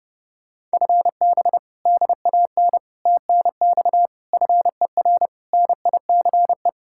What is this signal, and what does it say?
Morse code, heard as a keyed tone, about 31 wpm, 700 Hz. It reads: FB DAN TNX FER NICE